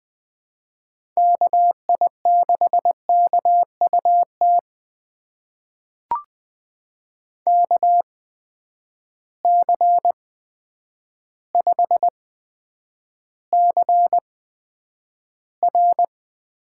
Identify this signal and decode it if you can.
Morse code, heard as a keyed tone, about 20 wpm, spaced as for 9 wpm, 700 Hz. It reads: KI6KUT KC5CR
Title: USA Call Signs - 6 Characters - 8x Character Spacing - 20wpm